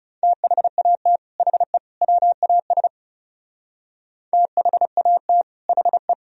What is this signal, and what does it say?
Morse code, heard as a keyed tone, about 36 wpm, 700 Hz. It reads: THAT HE WAS THAT HE
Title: Speed-Racing - Sets of 3 Words 30wpm